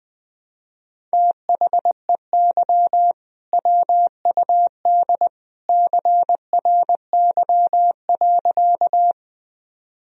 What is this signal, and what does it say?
Morse code, heard as a keyed tone, about 20 wpm, 700 Hz. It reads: THEY WUD CRY.
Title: Sentences from Top 500 Words and Abbreviations 20wpm